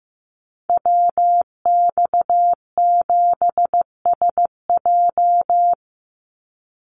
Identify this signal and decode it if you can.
Morse code, heard as a keyed tone, about 15 wpm, 700 Hz. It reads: WX7SJ
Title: Call Signs Repeated 2x - 15wpm